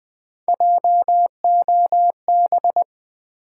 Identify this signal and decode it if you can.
Morse code, heard as a keyed tone, about 20 wpm, 700 Hz. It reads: JOB
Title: Sentences from Top 600 Words - New Word Drill - 20wpm